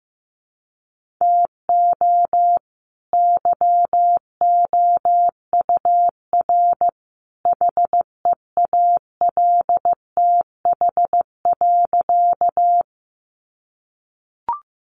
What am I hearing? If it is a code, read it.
Morse code, heard as a keyed tone, about 15 wpm, 700 Hz. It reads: TO YOUR HEALTH.
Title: Sentences from Top 600 Words 15wpm